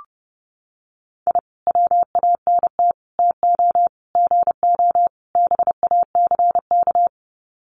Morse code, heard as a keyed tone, about 30 wpm, 700 Hz. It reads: I WANT TO GO BACK
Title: Sets of 5 Words Encore 30wpm